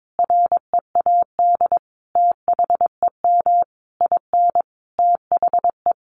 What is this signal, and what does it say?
Morse code, heard as a keyed tone, about 22 wpm, 700 Hz. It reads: READ THEM IN THE
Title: Sentences from Top 900 Words 22wpm